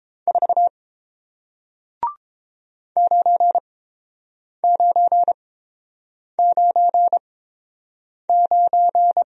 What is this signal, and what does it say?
Morse code, heard as a keyed tone, about 30 wpm, spaced as for 7 wpm, 700 Hz. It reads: 4 9999